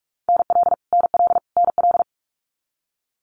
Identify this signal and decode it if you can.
Morse code, heard as a keyed tone, about 46 wpm, 700 Hz. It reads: NL NL NL
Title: Canadian Provinces and Territories - Repeated 3x - 45wpm